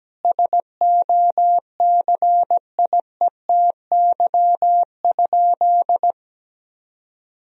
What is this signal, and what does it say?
Morse code, heard as a keyed tone, about 17 wpm, 700 Hz. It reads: SOCIETY?
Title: Sentences from Top 1000 Words - Review Part 4 - 17wpm